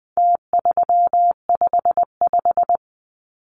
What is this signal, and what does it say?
Morse code, heard as a keyed tone, about 20 wpm, 700 Hz. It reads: T355
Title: Numbers - Contest Style - 4-Digit 20wpm